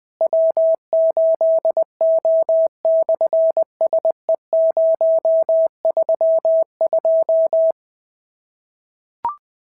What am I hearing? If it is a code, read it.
Morse code, heard as a keyed tone, about 19 wpm, 650 Hz. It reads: W8O/SE032
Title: SOTA QSO Practice 20wpm